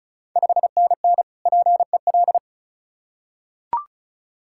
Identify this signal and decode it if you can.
Morse code, heard as a keyed tone, about 35 wpm, 690 Hz. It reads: 5NN PEL
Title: Ontario QSO Party Exchange 35wpm